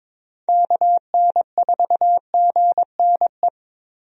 Morse code, heard as a keyed tone, about 22 wpm, 700 Hz. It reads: KN4GNE
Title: USA Call Signs - 6 Characters - 22wpm